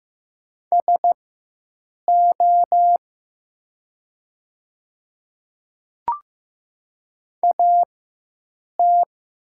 Morse code, heard as a keyed tone, about 15 wpm, spaced as for 7 wpm, 700 Hz. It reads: SO AT
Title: 4x Character Spacing - Top 100 Words - 15wpm